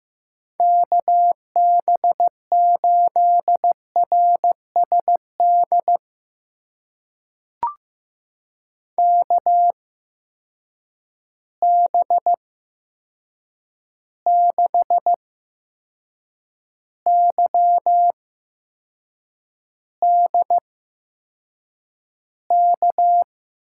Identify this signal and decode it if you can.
Morse code, heard as a keyed tone, about 15 wpm, spaced as for 6 wpm, 700 Hz. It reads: KB8RSD KB6YDK
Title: USA Call Signs - 6 Characters - 8x Character Spacing - 15wpm